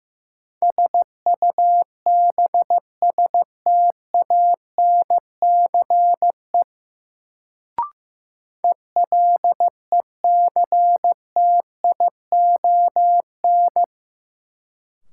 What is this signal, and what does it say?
Morse code, heard as a keyed tone, about 16 wpm, 700 Hz. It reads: SUBSTANCE ELECTION
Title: Sentences from Top 1500 Words - New Word Drill - 15wpm